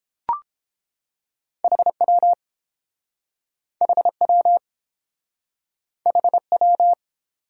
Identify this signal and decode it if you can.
Morse code, heard as a keyed tone, about 30 wpm, 700 Hz. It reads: HW HW HW